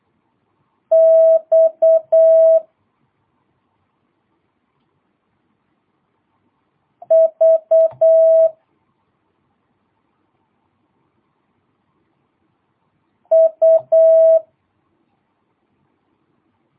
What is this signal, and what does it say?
Morse code, heard as a keyed tone, about 8 wpm, 650 Hz. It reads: X V U
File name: jh8cbh / 2022cw